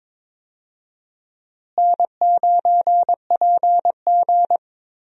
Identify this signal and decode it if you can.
Morse code, heard as a keyed tone, about 21 wpm, 700 Hz. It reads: N9PG
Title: USA Call Signs - 4 Characters - Repeated 2x - 22wpm